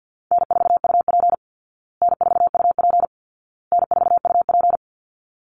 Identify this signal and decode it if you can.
Morse code, heard as a keyed tone, about 50 wpm, 700 Hz. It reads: N4UP N4UP N4UP
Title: Call Signs Repeated 2x - 50wpm